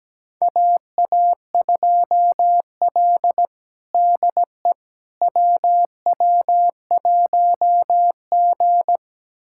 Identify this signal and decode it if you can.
Morse code, heard as a keyed tone, about 17 wpm, 700 Hz. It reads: AA2L DE WW1G
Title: USA Callsign DE Callsign - 4 Characters - 17wpm